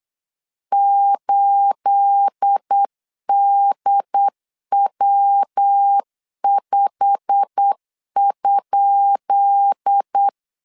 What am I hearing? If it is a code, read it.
Morse code, heard as a keyed tone, about 8 wpm, 800 Hz. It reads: 8DW5?
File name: LEZIONI / 10 SIGNS / 10B INT 8wpm 3534